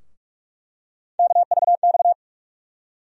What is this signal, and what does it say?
Morse code, heard as a keyed tone, about 46 wpm, 700 Hz. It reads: KVX
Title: USA Call Sign Postfixes - 3 Letters 45wpm